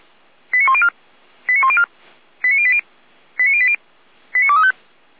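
Five-tone selective call (ZVEI), heard as quick runs of five tones, tones of about 70 ms. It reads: 89104 89104 89080 89080 89235
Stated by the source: ZVEI five-tone selective call received on an SDR.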